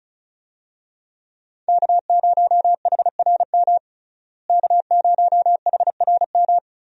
Morse code, heard as a keyed tone, about 36 wpm, 700 Hz. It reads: K0HRM K0HRM
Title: USA Call Signs - 5 Characters - Repeated 2x - 35wpm